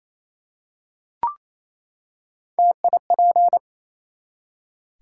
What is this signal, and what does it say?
Morse code, heard as a keyed tone, about 28 wpm, 700 Hz. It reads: TIP
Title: Three Letter Words - 28wpm